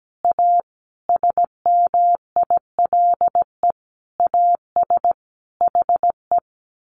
Morse code, heard as a keyed tone, about 17 wpm, 700 Hz. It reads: A SMILE AS HE
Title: A Sound Of Thunder 17wpm